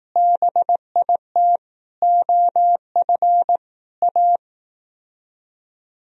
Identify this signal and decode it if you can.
Morse code, heard as a keyed tone, about 18 wpm, 700 Hz. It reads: BIT OF A
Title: Speed-Racing - Sets of 4 Words 15wpm